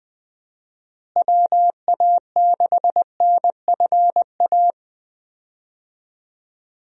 Morse code, heard as a keyed tone, about 20 wpm, 700 Hz. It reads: WA6NFA
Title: USA Call Signs - 6 Characters - Repeated 2x - 20wpm